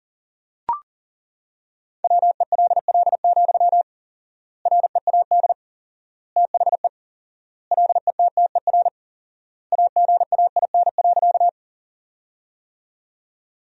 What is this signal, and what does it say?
Morse code, heard as a keyed tone, about 40 wpm, 700 Hz. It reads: WELL, READ THE LETTER AGAIN.
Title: 4x Word-Spacing - Sentences from Top 300 Words - 40wpm